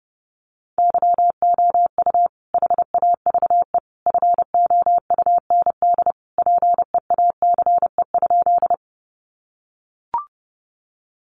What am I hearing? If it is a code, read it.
Morse code, heard as a keyed tone, about 30 wpm, 700 Hz. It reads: YOU HAVE FOUND PEACE?
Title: Sentences from Top 1500 Words 30wpm